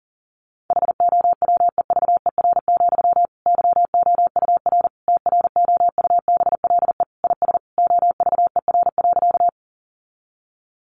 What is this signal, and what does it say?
Morse code, heard as a keyed tone, about 40 wpm, 700 Hz. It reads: HOWEVER, YOUR TROUBLE IS OVER.
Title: Sentences from Top 1200 Words 40wpm